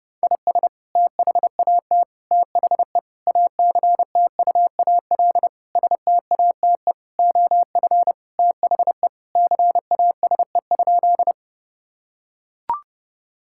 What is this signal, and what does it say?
Morse code, heard as a keyed tone, about 30 wpm, 700 Hz. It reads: IS THAT THE ACTUAL STATE OF THE CASE?
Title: Sentences from Top 1000 Words - Review Part 5 - 30wpm